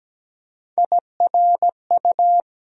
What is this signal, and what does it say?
Morse code, heard as a keyed tone, about 17 wpm, 700 Hz. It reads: IRU
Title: USA Call Sign Postfixes - 3 Letters 17wpm